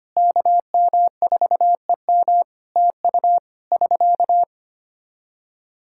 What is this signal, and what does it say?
Morse code, heard as a keyed tone, about 25 wpm, 700 Hz. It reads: KM4EM TU <SK>
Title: Practice QSO - Exchange 4 - Ending - 25wpm